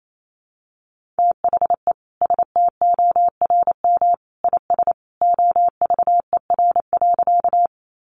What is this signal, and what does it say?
Morse code, heard as a keyed tone, about 28 wpm, 700 Hz. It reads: THE STORM IS OVER.
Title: Sentences from Top 1500 Words 28wpm